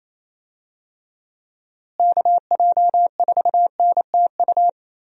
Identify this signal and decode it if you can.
Morse code, heard as a keyed tone, about 27 wpm, 700 Hz. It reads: KJ4NTU
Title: USA Call Signs - 6 Characters - Repeated 3x - 28wpm